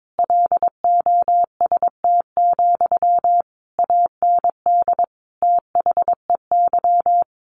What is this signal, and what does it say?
Morse code, heard as a keyed tone, about 22 wpm, 700 Hz. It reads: LOST, AND THEY
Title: Sentences from Top 1500 Words 22wpm